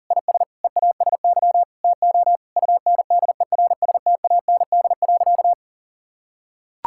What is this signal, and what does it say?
Morse code, heard as a keyed tone, about 40 wpm, 700 Hz. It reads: IS EASY TO UNDERSTAND.
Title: Sentences from Top 1000 Words - Review Part 6 - 40wpm